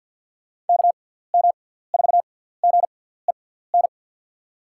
Morse code, heard as a keyed tone, about 47 wpm, spaced as for 17 wpm, 700 Hz. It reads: KM4GEN
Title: USA Call Signs - 6 Characters - 6x Character Spacing - 50wpm